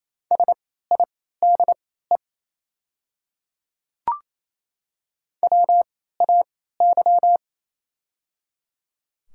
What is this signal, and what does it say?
Morse code, heard as a keyed tone, about 28 wpm, spaced as for 16 wpm, 700 Hz. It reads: SIDE WAY
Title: 3x Character Spacing - Top 100 Words - 28wpm